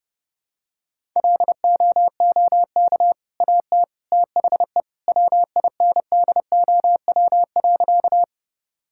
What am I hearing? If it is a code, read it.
Morse code, heard as a keyed tone, about 32 wpm, 700 Hz. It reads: LOOK AT THE WINDOW.
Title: Sentences from Top 1600 Words 30wpm